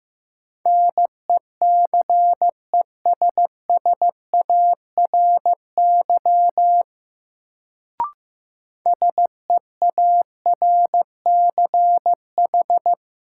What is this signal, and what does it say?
Morse code, heard as a keyed tone, about 15 wpm, 700 Hz. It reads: NECESSARY SEARCH
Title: Sentences from Top 800 Words - New Word Drill - 15wpm